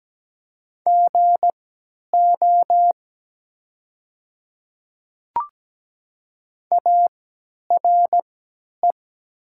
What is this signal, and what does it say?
Morse code, heard as a keyed tone, about 17 wpm, spaced as for 10 wpm, 700 Hz. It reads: GO ARE
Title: 3x Character Spacing - Top 300 Words - 17wpm